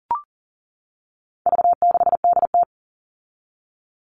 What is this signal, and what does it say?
Morse code, heard as a keyed tone, about 40 wpm, 700 Hz. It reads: V6DT
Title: All Cut Numbers - 4-Digit 40wpm